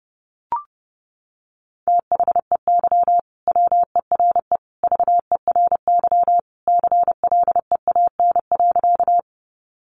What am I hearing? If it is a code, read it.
Morse code, heard as a keyed tone, about 30 wpm, 700 Hz. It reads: THEY WERE VERY CLEAN.
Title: Sentences from Top 1100 Words 30wpm